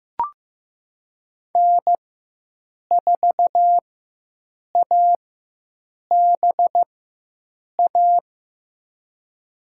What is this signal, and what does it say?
Morse code, heard as a keyed tone, about 15 wpm, spaced as for 7 wpm, 700 Hz. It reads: N4ABA
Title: USA Call Signs - 5 Characters - 4x Character Spacing - 15wpm